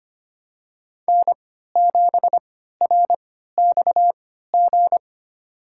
Morse code, heard as a keyed tone, about 25 wpm, spaced as for 14 wpm, 700 Hz. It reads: N7RXG
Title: USA Call Signs - 3x Character Spacing - 25wpm